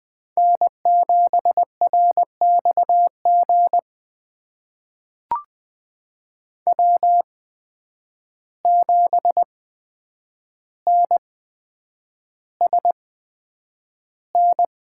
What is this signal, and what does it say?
Morse code, heard as a keyed tone, about 20 wpm, spaced as for 9 wpm, 700 Hz. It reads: N7RXG W7NSN
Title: USA Call Signs - 5 Characters - 8x Character Spacing - 20wpm